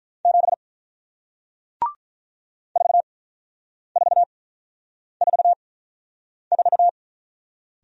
Morse code, heard as a keyed tone, about 48 wpm, spaced as for 9 wpm, 700 Hz. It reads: 7 4444